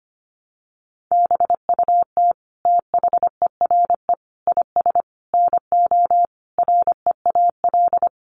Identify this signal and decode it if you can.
Morse code, heard as a keyed tone, about 25 wpm, 700 Hz. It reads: BUT THERE IS NO REAL
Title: Sentences from Top 1300 Words 25wpm